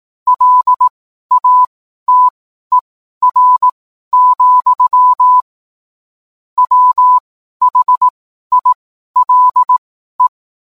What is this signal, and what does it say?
Morse code, heard as a keyed tone, about 18 wpm, spaced as for 12 wpm, 1000 Hz. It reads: LATER, WHILE